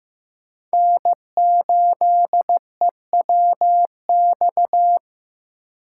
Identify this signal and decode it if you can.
Morse code, heard as a keyed tone, about 15 wpm, 700 Hz. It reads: N8EWX
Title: K1USN SST QSO Practice - 15wpm